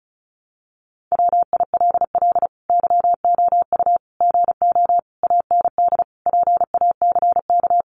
Morse code, heard as a keyed tone, about 34 wpm, 700 Hz. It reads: WILL YOU GO AND PACK
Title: Sentences from Top 1300 Words 35wpm